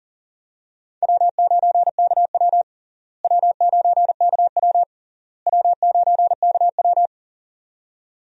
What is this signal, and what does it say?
Morse code, heard as a keyed tone, about 40 wpm, 700 Hz. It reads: W9KW W9KW W9KW
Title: USA Call Signs - 4 Characters - Repeated 3x - 40wpm